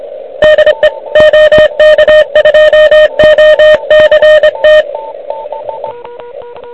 Morse code, heard as a keyed tone, about 24 wpm, 1850 Hz, with noise. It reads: DE OK2OCT
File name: ARCHIV / QCX / QCX Pileup